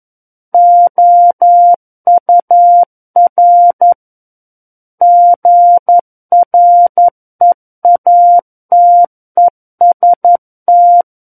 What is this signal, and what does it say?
Morse code, heard as a keyed tone, about 11 wpm, 700 Hz. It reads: OUR GREATEST
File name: Book One / 02-Chapter 2 - 11 WPM